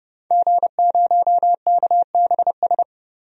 Morse code, heard as a keyed tone, about 29 wpm, 700 Hz. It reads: G0KBS
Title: DXCC-100 Call Signs - 5 Characters - 30wpm